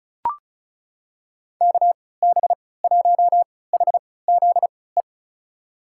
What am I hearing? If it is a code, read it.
Morse code, heard as a keyed tone, about 36 wpm, spaced as for 20 wpm, 700 Hz. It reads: KB1HZE
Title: USA Call Signs - 6 Characters - 3x Character Spacing - 35wpm